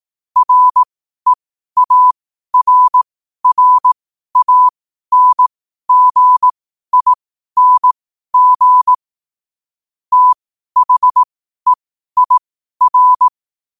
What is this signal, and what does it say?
Morse code, heard as a keyed tone, about 18 wpm, spaced as for 12 wpm, 1000 Hz. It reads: REARRANGING THEIR
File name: qotdfiles / 12wpm